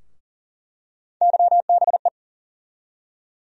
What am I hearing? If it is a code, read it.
Morse code, heard as a keyed tone, about 40 wpm, 700 Hz. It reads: YBE
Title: USA Call Sign Postfixes - 3 Letters 40wpm